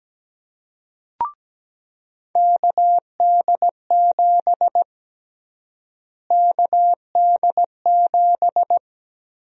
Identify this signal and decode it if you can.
Morse code, heard as a keyed tone, about 17 wpm, 700 Hz. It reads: KD7 KD7
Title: USA Call Sign Prefixes - Repeated 2x - 17wpm